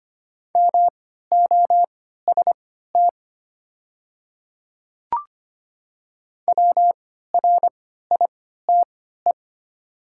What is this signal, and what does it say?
Morse code, heard as a keyed tone, about 25 wpm, spaced as for 14 wpm, 700 Hz. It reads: MOST WRITE
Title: 3x Character Spacing - Top 100 Words - 25wpm